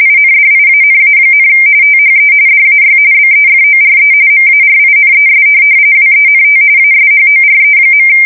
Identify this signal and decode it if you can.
RTTY, about 45.45 baud. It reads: INDIANA, THE DAY BEFORE MARINES WENT ASHORE.